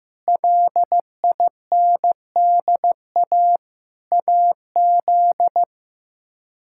Morse code, heard as a keyed tone, about 15 wpm, 700 Hz. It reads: LINDA AZ
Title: K1USN SST QSO Practice - 15wpm